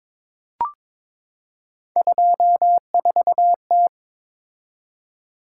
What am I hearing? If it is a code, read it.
Morse code, heard as a keyed tone, about 22 wpm, 700 Hz. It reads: 24T
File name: practice / 22wpm / Three Letter-Number Combinations - 22wpm